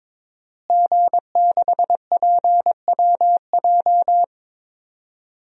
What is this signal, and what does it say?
Morse code, heard as a keyed tone, about 22 wpm, 700 Hz. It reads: G6PWJ
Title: DXCC-100 Call Signs - 5 Characters - Repeated 3x - 22wpm